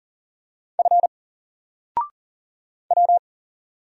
Morse code, heard as a keyed tone, about 39 wpm, 700 Hz. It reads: F W
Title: Single Character (Letter:Number:Punctuation:Prosign) 40wpm